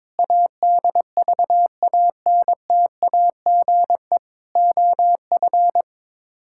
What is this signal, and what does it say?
Morse code, heard as a keyed tone, about 22 wpm, 700 Hz. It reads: ADVANTAGE OF
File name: practice / 22wpm / Build A Phrase - 5-Word Phrases - 22wpm